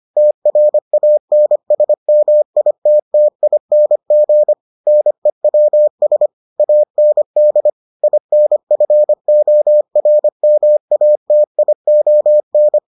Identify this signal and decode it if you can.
Morse code, heard as a keyed tone, about 25 wpm, 600 Hz. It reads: TRANSMITTING NEWS AND INFORMATION